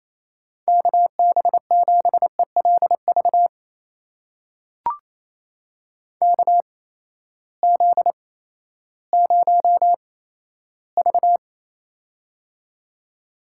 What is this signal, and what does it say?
Morse code, heard as a keyed tone, about 28 wpm, spaced as for 14 wpm, 700 Hz. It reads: KB7ELV KZ0V
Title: USA Call Signs - 8x Character Spacing - 28wpm